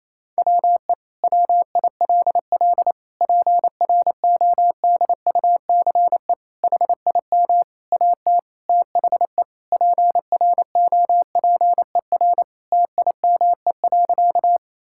Morse code, heard as a keyed tone, about 28 wpm, 700 Hz. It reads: WE WILL PRODUCE HIM AT THE PROPER TIME.
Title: Sentences from Top 1000 Words - Review Part 5 - 28wpm